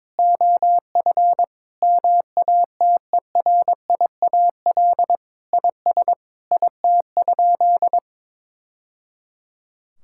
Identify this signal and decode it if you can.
Morse code, heard as a keyed tone, about 22 wpm, 700 Hz. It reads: OF MATERIAL IS IT?
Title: Sentences from Top 1000 Words - Review Part 5 - 22wpm